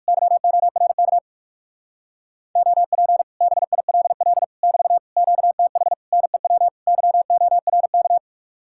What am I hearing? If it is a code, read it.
Morse code, heard as a keyed tone, about 57 wpm, 700 Hz. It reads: YORK OP BILL <BT> QTH NEW YORK